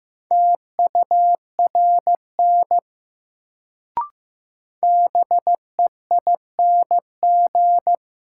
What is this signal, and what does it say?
Morse code, heard as a keyed tone, about 15 wpm, 700 Hz. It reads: TURN BEING
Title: Sets of 4 Words - New Word Drill - 15wpm